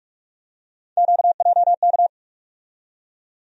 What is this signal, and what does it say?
Morse code, heard as a keyed tone, about 46 wpm, 700 Hz. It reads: QJK
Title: USA Call Sign Postfixes - 3 Letters 45wpm